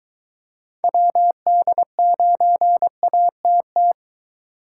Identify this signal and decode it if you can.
Morse code, heard as a keyed tone, about 22 wpm, 700 Hz. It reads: WD9ATT